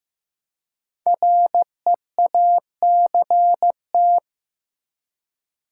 Morse code, heard as a keyed tone, about 15 wpm, 700 Hz. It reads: REACT
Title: Five Letter Words - 15wpm